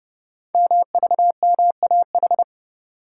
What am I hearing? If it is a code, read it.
Morse code, heard as a keyed tone, about 29 wpm, 700 Hz. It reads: MVMAH
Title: Code Groups - 5 Letters - 30wpm